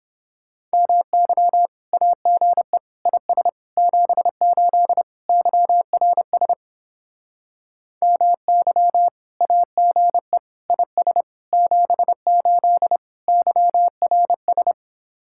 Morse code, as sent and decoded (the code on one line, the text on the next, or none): -- -.-- / .- --. / .. ... / --... ---.. / -.-- .-. ... / -- -.-- / .- --. / .. ... / --... ---.. / -.-- .-. ...
MY AGE IS 78 YRS MY AGE IS 78 YRS